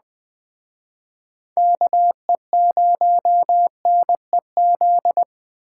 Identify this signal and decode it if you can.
Morse code, heard as a keyed tone, about 21 wpm, 700 Hz. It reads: KE0NEZ
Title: Speed Racing - USA 2x3 Call Signs 17wpm